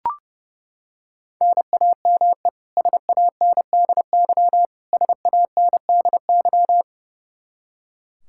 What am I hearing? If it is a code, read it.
Morse code, heard as a keyed tone, about 30 wpm, 700 Hz. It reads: NAME SANDY SANDY
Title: Practice QSO - Exchange 1 - Beginning - 30wpm